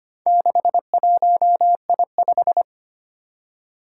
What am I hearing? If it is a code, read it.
Morse code, heard as a keyed tone, about 25 wpm, 700 Hz. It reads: 61I5
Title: Four Challenging Letter-Number Combinations 25wpm